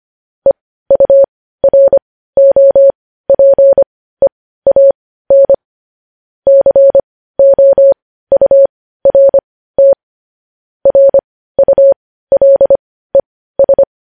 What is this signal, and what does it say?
Morse code, heard as a keyed tone, about 25 wpm, spaced as for 15 wpm, 550 Hz. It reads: EUROPEAN COURT RULES
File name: podcast / News headlines-1x-25x15